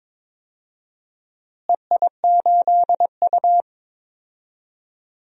Morse code, heard as a keyed tone, about 22 wpm, 700 Hz. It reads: EI8U
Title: DXCC-100 Call Signs - 4 Characters - Repeated 2x - 22wpm